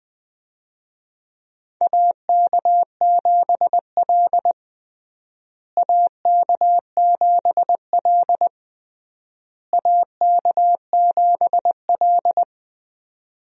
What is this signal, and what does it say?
Morse code, heard as a keyed tone, about 21 wpm, 700 Hz. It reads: AK7L AK7L AK7L
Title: USA Call Signs - 4 Characters - Repeated 3x - 20wpm